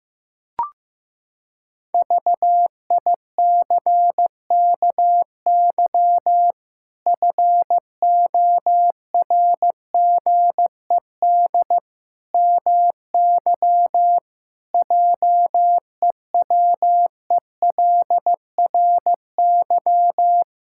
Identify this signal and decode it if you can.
Morse code, heard as a keyed tone, about 16 wpm, 700 Hz. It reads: VICKY FORGED MY JEWELRY
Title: Pangram Challenge v2 - 15wpm